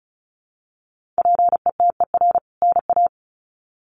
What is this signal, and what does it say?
Morse code, heard as a keyed tone, about 34 wpm, 700 Hz. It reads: PETER NA
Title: CWT Exchange 35wpm